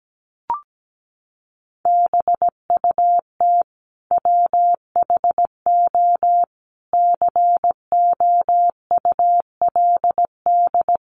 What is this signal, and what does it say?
Morse code, heard as a keyed tone, about 17 wpm, 700 Hz. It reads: BUT WHO COULD
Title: Sentences from Top 1200 Words 17wpm